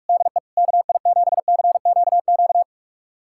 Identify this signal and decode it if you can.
Morse code, heard as a keyed tone, about 44 wpm, 700 Hz. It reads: DE KI7CQQ
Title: USA Callsign DE Callsign 45wpm